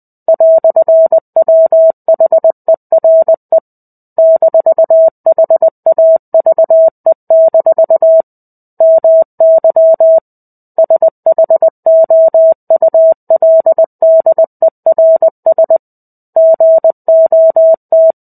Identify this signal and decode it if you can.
Morse code, heard as a keyed tone, about 20 wpm, 650 Hz. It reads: "WHERE -HAVE- MY SHOULDERS GOT
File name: 05-Chapter-05 20 WPM